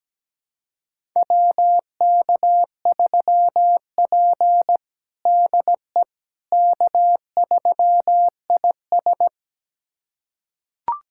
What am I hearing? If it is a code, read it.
Morse code, heard as a keyed tone, about 17 wpm, 700 Hz. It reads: WK3P DE K3IS
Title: USA Callsign DE Callsign - 4 Characters - 17wpm